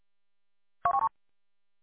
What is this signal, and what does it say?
There is no code.